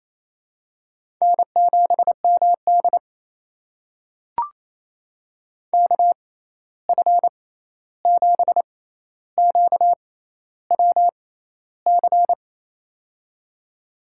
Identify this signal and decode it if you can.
Morse code, heard as a keyed tone, about 28 wpm, spaced as for 13 wpm, 700 Hz. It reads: N7MD KF7QWC
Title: USA Call Signs - 6x Character Spacing - 28wpm